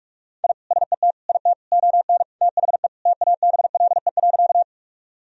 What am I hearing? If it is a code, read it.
Morse code, heard as a keyed tone, about 46 wpm, 700 Hz. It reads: I SET IT ON THE TABLE.
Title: Sentences from Top 1000 Words - Review Part 2 - 45wpm